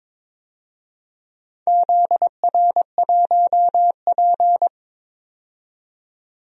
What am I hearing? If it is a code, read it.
Morse code, heard as a keyed tone, about 22 wpm, 700 Hz. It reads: ZR1P
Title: DXCC-100 Call Signs - 4 Characters - 22wpm